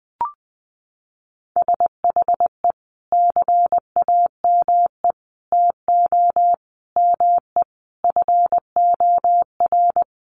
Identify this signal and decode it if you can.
Morse code, heard as a keyed tone, about 19 wpm, 700 Hz. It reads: SHE CAME TO ME FOR
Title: Sentences from Top 1100 Words 20wpm